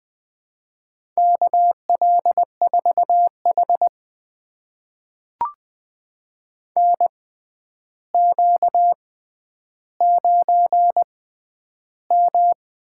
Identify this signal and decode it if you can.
Morse code, heard as a keyed tone, about 20 wpm, spaced as for 10 wpm, 700 Hz. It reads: KL4H NQ9M
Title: USA Call Signs - 4 Characters - 6x Character Spacing - 20wpm